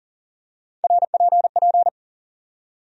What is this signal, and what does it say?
Morse code, heard as a keyed tone, about 40 wpm, 700 Hz. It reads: RPP
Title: USA Call Sign Postfixes - 3 Letters 40wpm